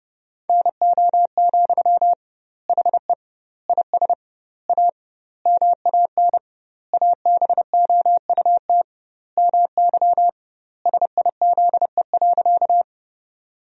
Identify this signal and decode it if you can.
Morse code, heard as a keyed tone, about 30 wpm, 700 Hz. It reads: NO, HE IS A MAN ABOUT MY SIZE.
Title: 2x Word-Spacing - Sentences from Top 400 Words - 30wpm